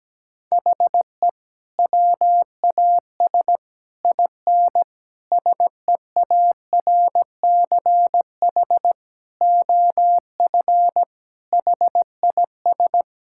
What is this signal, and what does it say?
Morse code, heard as a keyed tone, about 17 wpm, 700 Hz. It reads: HE WAS IN SEARCH OF HIS
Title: Sentences from Top 1000 Words - Review Part 5 - 17wpm